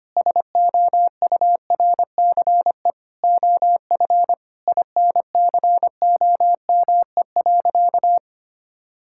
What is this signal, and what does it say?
Morse code, heard as a keyed tone, about 25 wpm, 700 Hz. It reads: SOURCE OF INCOME.
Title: Sentences from Top 1000 Words - Review Part 5 - 25wpm